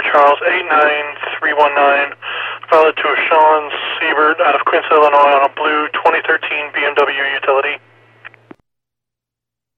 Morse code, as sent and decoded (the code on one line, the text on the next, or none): none